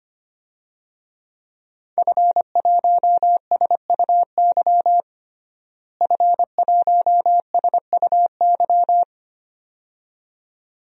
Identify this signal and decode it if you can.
Morse code, heard as a keyed tone, about 25 wpm, 700 Hz. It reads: F1SUY F1SUY
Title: DXCC-100 Call Signs - 5 Characters - Repeated 2x - 25wpm